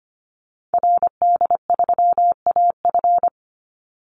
Morse code, heard as a keyed tone, about 25 wpm, 700 Hz. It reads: RD3AF
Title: Call Signs Repeated 2x - 25wpm